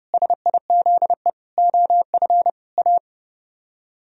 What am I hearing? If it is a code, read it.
Morse code, heard as a keyed tone, about 36 wpm, 700 Hz. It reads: SIZE OF A
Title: Speed-Racing - Sets of 4 Words 30wpm